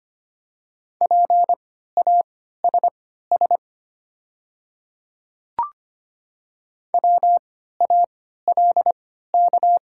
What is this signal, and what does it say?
Morse code, heard as a keyed tone, about 25 wpm, spaced as for 14 wpm, 700 Hz. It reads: PASS WALK
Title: 3x Character Spacing - Top 500 Words - 25wpm